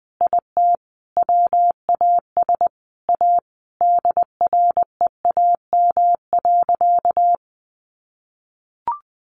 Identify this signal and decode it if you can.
Morse code, heard as a keyed tone, about 19 wpm, 700 Hz. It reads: IT WAS A DREAM.
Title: Sentences from Top 1500 Words 20wpm